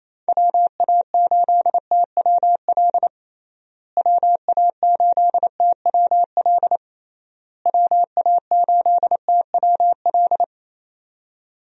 Morse code, heard as a keyed tone, about 28 wpm, 700 Hz. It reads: WA8TWL WA8TWL WA8TWL
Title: USA Call Signs - 6 Characters - Repeated 3x - 28wpm